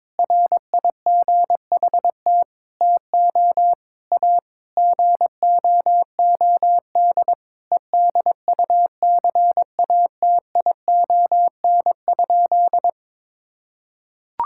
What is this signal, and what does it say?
Morse code, heard as a keyed tone, about 22 wpm, 700 Hz. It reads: RIGHT TO A GOOD EDUCATION?
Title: Sentences from Top 1000 Words - Review Part 6 - 22wpm